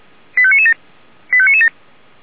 Five-tone selective call (ZVEI), heard as quick runs of five tones, tones of about 70 ms. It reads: 85997 85997